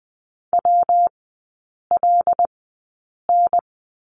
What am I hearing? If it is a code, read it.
Morse code, heard as a keyed tone, about 20 wpm, spaced as for 8 wpm, 700 Hz. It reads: WLN
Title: 178 Lesson 29e - Slash - Three Characters 20wpm